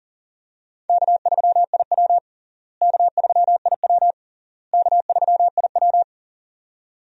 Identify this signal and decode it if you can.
Morse code, heard as a keyed tone, about 40 wpm, 700 Hz. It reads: K3IW K3IW K3IW
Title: USA Call Signs - 4 Characters - Repeated 3x - 40wpm